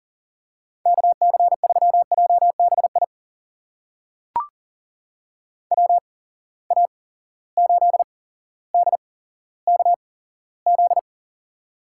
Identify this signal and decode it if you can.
Morse code, heard as a keyed tone, about 40 wpm, spaced as for 17 wpm, 700 Hz. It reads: KC3JBI WA8DKZ
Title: USA Call Signs - 6 Characters - 8x Character Spacing - 40wpm